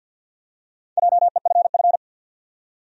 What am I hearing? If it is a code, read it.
Morse code, heard as a keyed tone, about 49 wpm, 700 Hz. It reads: JEFF
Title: USA Common Names - Max 4 Letters - 50wpm